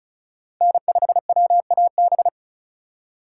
Morse code, heard as a keyed tone, about 36 wpm, 700 Hz. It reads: N5WAB
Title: USA Call Signs - 5 Characters - 2x Character Spacing - 35wpm